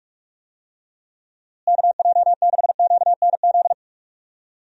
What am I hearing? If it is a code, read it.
Morse code, heard as a keyed tone, about 46 wpm, 700 Hz. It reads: KJ6QNZ
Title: USA Call Signs - 6 Characters - Repeated 2x - 45wpm